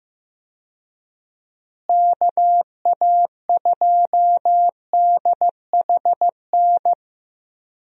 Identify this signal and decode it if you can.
Morse code, heard as a keyed tone, about 16 wpm, 700 Hz. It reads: KA2DHN
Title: USA Call Signs - 6 Characters - Repeated 2x - 15wpm